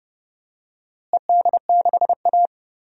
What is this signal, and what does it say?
Morse code, heard as a keyed tone, about 31 wpm, 700 Hz. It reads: ED6A
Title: DXCC-100 Call Signs - 4 Characters - Repeated 2x - 30wpm